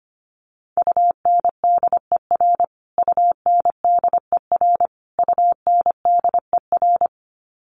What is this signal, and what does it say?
Morse code, heard as a keyed tone, about 25 wpm, 700 Hz. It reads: UNDER UNDER UNDER
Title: Top 200 Words Repeated 25wpm